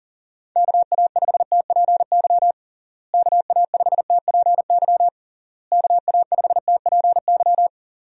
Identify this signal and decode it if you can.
Morse code, heard as a keyed tone, about 40 wpm, 700 Hz. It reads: KA5TPY KA5TPY KA5TPY
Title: USA Call Signs - 6 Characters - Repeated 3x - 40wpm